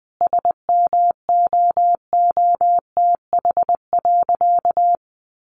Morse code, heard as a keyed tone, about 20 wpm, 700 Hz. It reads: SMOOTH.
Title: Sentences from Top 1400 Words 20wpm